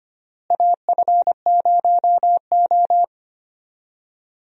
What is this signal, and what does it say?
Morse code, heard as a keyed tone, about 25 wpm, 700 Hz. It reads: AF0O
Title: USA Call Signs - 4 Characters - Repeated 3x - 25wpm